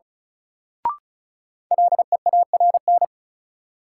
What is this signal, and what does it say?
Morse code, heard as a keyed tone, about 35 wpm, 700 Hz. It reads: LEARN
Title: Sentences from Top 300 Words - New Word Drill - 35wpm